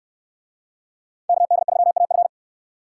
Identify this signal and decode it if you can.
Morse code, heard as a keyed tone, about 61 wpm, 700 Hz. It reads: KD3AF